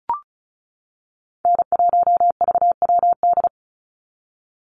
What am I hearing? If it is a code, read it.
Morse code, heard as a keyed tone, about 34 wpm, 700 Hz. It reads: N1VWD
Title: NA SOTA Chaser Callsigns 35wpm